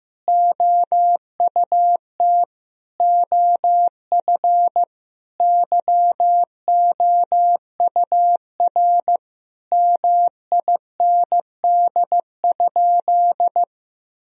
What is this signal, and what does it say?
Morse code, heard as a keyed tone, about 15 wpm, 700 Hz. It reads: OUT OF YOUR MIND?
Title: Sentences from Top 1000 Words - Review Part 4 - 15wpm